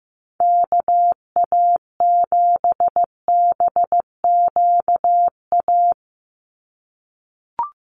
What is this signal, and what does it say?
Morse code, heard as a keyed tone, about 15 wpm, 700 Hz. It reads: KA7BQA
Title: USA Call Signs 15wpm